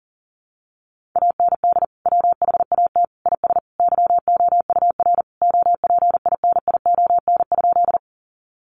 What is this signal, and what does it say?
Morse code, heard as a keyed tone, about 40 wpm, 700 Hz. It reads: AND WHAT IS YOUR OPINION?
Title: Sentences from Top 1200 Words 40wpm